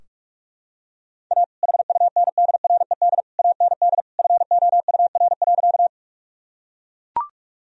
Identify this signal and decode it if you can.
Morse code, heard as a keyed tone, about 41 wpm, 700 Hz. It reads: A HUNDRED AND FOUR.
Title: Sentences from Top 1000 Words - Review Part 2 - 45wpm